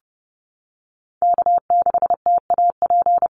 Morse code, heard as a keyed tone, about 30 wpm, 700 Hz. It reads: K6TAP
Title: USA Call Signs Repeated 2x 30wpm